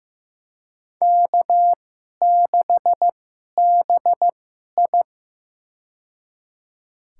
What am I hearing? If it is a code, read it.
Morse code, heard as a keyed tone, about 15 wpm, spaced as for 11 wpm, 700 Hz. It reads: K6BI
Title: USA Call Signs - 4 Characters - 2x Character Spacing - 15wpm